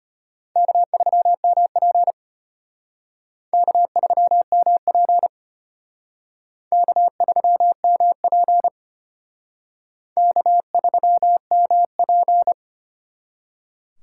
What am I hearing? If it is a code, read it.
Morse code, heard as a keyed tone, about 33 wpm, 700 Hz. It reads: K3MP K3MP K3MP K3MP